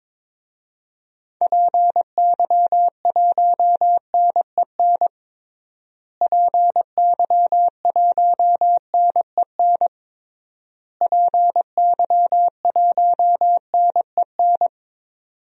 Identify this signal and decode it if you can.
Morse code, heard as a keyed tone, about 22 wpm, 700 Hz. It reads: PY1NEN PY1NEN PY1NEN
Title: DXCC-100 Call Signs - 6 Characters - Repeated 3x - 22wpm